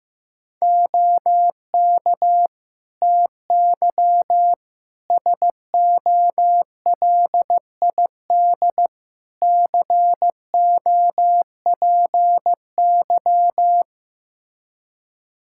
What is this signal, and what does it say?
Morse code, heard as a keyed tone, about 15 wpm, 700 Hz. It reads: OK TY SOLID COPY
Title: K1USN SST QSO Practice 15wpm